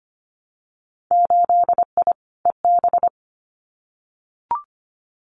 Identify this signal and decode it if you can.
Morse code, heard as a keyed tone, about 25 wpm, 700 Hz. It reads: 8I EB
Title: Winter Field Day Exchanges Encore 25wpm